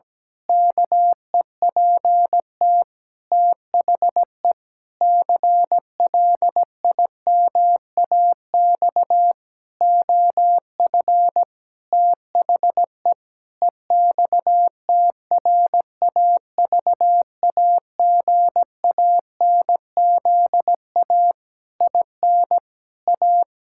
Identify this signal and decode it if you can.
Morse code, heard as a keyed tone, about 17 wpm, 700 Hz. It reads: KEPT THE CLIMAX OF THE EXTRAVAGANZA IN A
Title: Pangram Challenge v2 - 17wpm